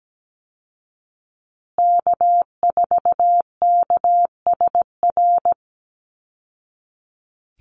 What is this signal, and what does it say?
Morse code, heard as a keyed tone, about 17 wpm, 700 Hz. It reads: K4KSR
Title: Call Signs Repeated 3x - 17wpm